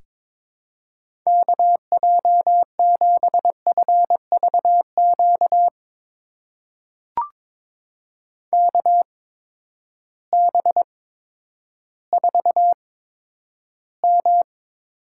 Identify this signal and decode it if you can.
Morse code, heard as a keyed tone, about 22 wpm, spaced as for 11 wpm, 700 Hz. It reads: KJ7FVQ KB4M
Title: USA Call Signs - 6 Characters - 8x Character Spacing - 22wpm